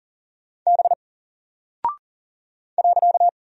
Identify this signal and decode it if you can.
Morse code, heard as a keyed tone, about 40 wpm, 700 Hz. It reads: B .
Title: Single Character (Letter:Number:Punctuation:Prosign) 40wpm